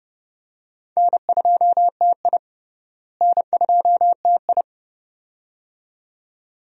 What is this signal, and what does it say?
Morse code, heard as a keyed tone, about 30 wpm, 700 Hz. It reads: N2TI N2TI
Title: USA Call Signs - 4 Characters - Repeated 2x - 30wpm